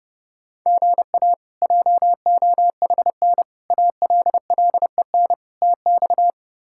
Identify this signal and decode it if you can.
Morse code, heard as a keyed tone, about 30 wpm, 700 Hz. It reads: GA JOHN ALLEN TX
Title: K1USN SST QSO Practice - 30wpm